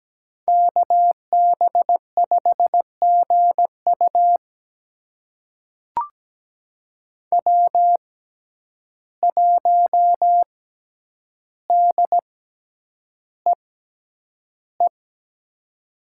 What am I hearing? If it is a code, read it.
Morse code, heard as a keyed tone, about 17 wpm, spaced as for 9 wpm, 700 Hz. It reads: KB5GU W1DEE
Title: USA Call Signs - 5 Characters - 6x Character Spacing - 17wpm